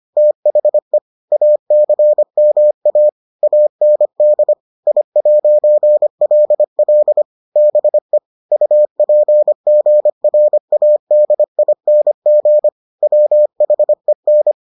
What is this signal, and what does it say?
Morse code, heard as a keyed tone, about 25 wpm, 600 Hz. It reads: THE ACMA AND I'LL BE UPGRADING WHEN